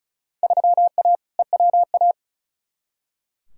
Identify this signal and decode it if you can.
Morse code, heard as a keyed tone, about 36 wpm, 700 Hz. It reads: 3A EWA
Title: Field Day Exchanges Encore - 35wpm